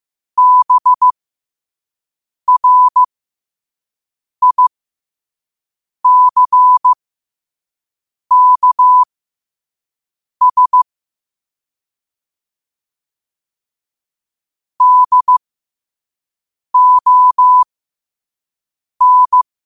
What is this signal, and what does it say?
Morse code, heard as a keyed tone, about 15 wpm, spaced as for 5 wpm, 1000 Hz. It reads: BRICKS DON
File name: qotdfiles / 5wpm